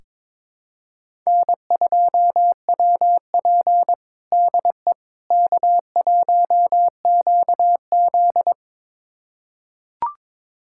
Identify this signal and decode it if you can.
Morse code, heard as a keyed tone, about 22 wpm, 700 Hz. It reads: N2WP DE K1QZ
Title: USA Callsign DE Callsign - 4 Characters - 22wpm